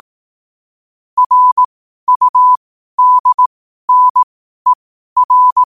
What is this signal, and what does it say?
Morse code, heard as a keyed tone, about 18 wpm, spaced as for 12 wpm, 1000 Hz. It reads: RUDNER
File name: qotdfiles / 12wpm